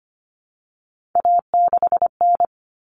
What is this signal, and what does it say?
Morse code, heard as a keyed tone, about 25 wpm, 700 Hz. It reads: A6N
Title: All Cut Numbers - 3-Digit 25wpm